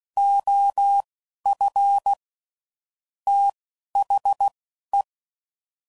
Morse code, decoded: OF THE

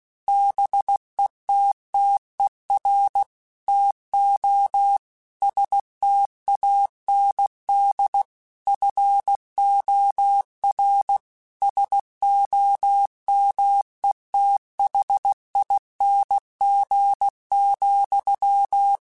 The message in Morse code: BETTER TO STAND FOR SOMETHING,

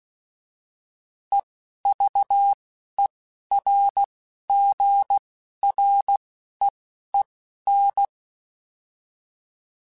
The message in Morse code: EVERGREEN

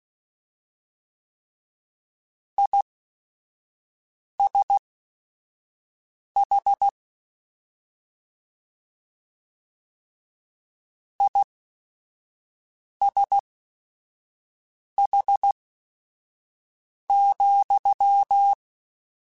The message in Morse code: ISH ISH,

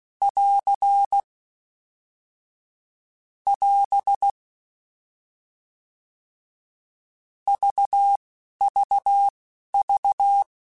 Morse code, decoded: <AR> <AS> VVV

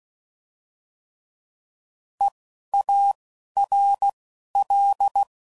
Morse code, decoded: EARL

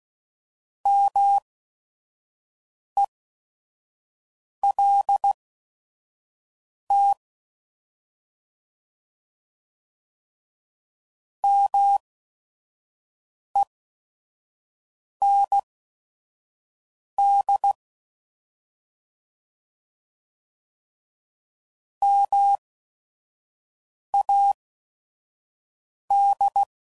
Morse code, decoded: MELT MEND MAD

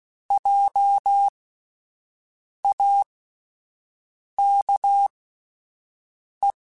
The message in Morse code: JAKE

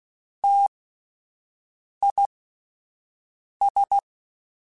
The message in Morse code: TIS

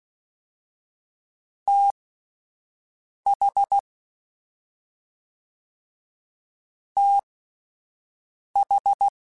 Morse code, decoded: TH TH

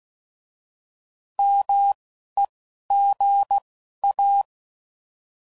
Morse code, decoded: MEGA

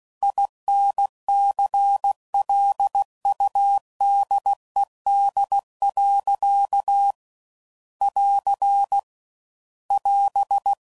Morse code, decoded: INCLUDED. <AR> <AS>

